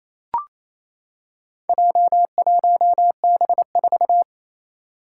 Morse code, heard as a keyed tone, about 28 wpm, 700 Hz. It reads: J1B4
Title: Four Challenging Letter-Number Combinations 28wpm